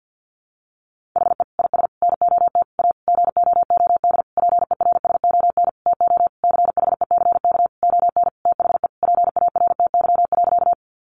Morse code, heard as a keyed tone, about 50 wpm, 700 Hz. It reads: HE IS NOT A GOOD PERSON TO CHECK ON THE PARTY.
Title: Sentences from Top 1400 Words 50wpm